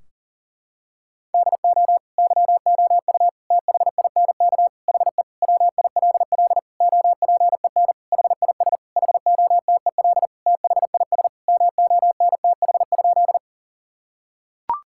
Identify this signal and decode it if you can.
Morse code, heard as a keyed tone, about 40 wpm, 700 Hz. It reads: DO YOU THINK HE WILL OPEN HIS HOTEL THIS MONTH?
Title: Sentences from Top 1600 Words 40wpm